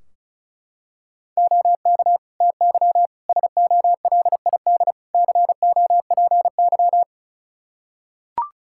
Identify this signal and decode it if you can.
Morse code, heard as a keyed tone, about 36 wpm, 700 Hz. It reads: OK TY SOLID COPY
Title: K1USN SST QSO Practice 35wpm